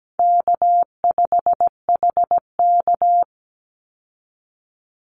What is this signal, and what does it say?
Morse code, heard as a keyed tone, about 17 wpm, 700 Hz. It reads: K5HK
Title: Call Signs Repeated 2x - 17wpm